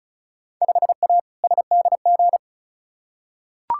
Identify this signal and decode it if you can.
Morse code, heard as a keyed tone, about 34 wpm, 700 Hz. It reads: 5A SDG
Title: Field Day Exchanges Encore - 35wpm